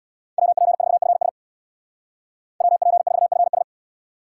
Morse code, heard as a keyed tone, about 63 wpm, 700 Hz. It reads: WP4FS WP4FS